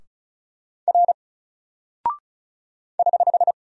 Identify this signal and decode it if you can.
Morse code, heard as a keyed tone, about 36 wpm, 700 Hz. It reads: R <HH>
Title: Single Character (Letter:Number:Punctuation:Prosign) 35wpm